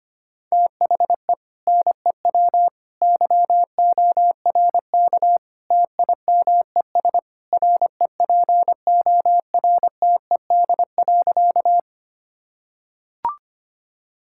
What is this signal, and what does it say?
Morse code, heard as a keyed tone, about 25 wpm, 700 Hz. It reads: THE NEW YORK TIMES REPORTED.
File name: practice / 25wpm / Sentences with 3-Word Phrases - 25wpm